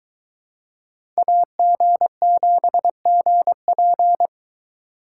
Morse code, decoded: AG7GP

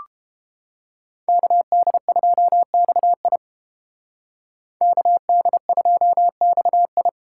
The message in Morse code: KD2XI KD2XI

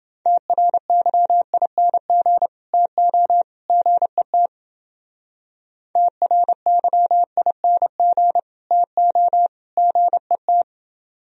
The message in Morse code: TRYING TO GET TRYING TO GET

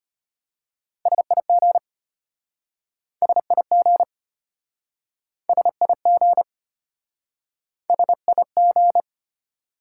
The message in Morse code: SIG SIG SIG SIG

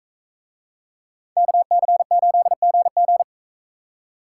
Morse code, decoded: KC8GG

